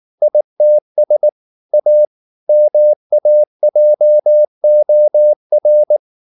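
Morse code, heard as a keyed tone, about 19 wpm, 600 Hz. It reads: ITS A MAJOR
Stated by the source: Morse code at 19 wpm from CW med Gnister, reading Ken Follett's novel Never in English.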